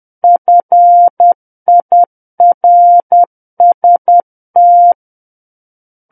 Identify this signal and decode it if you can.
Morse code, heard as a keyed tone, about 10 wpm, 700 Hz. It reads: FIRST
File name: Book One / 01-Chapter 1 - 10 WPM